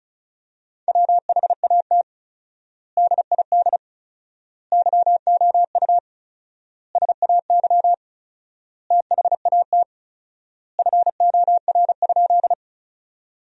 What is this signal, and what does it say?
Morse code, heard as a keyed tone, about 36 wpm, 700 Hz. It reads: WHAT DID YOU SAY THAT FOR?
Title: 4x Word-Spacing - Sentences from Top 200 Words - 35wpm